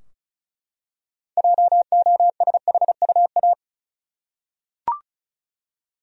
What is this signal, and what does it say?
Morse code, heard as a keyed tone, about 36 wpm, 700 Hz. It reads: JOSHUA
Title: USA Common Names - 35wpm